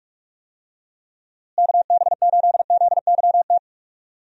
Morse code, heard as a keyed tone, about 46 wpm, 700 Hz. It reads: KB8ZYT
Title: USA Call Signs - 6 Characters - Repeated 3x - 45wpm